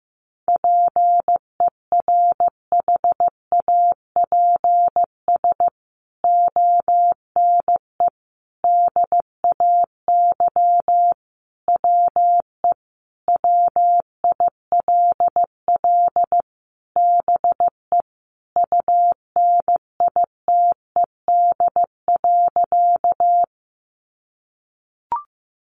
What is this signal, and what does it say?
Morse code, heard as a keyed tone, about 14 wpm, 700 Hz. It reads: PERHAPS ONE DAY WE WILL BE UNITED.
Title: Sentences from Top 600 Words 15wpm